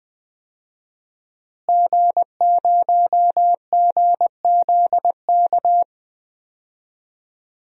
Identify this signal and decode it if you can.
Morse code, heard as a keyed tone, about 20 wpm, 700 Hz. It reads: G0GZK
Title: DXCC-100 Call Signs - 5 Characters - Repeated 3x - 20wpm